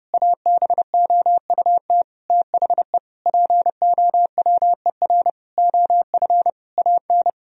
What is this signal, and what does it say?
Morse code, heard as a keyed tone, about 30 wpm, 700 Hz. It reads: ABOUT THE POWER OF AN
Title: Sentences from Top 1000 Words - Review Part 4 - 30wpm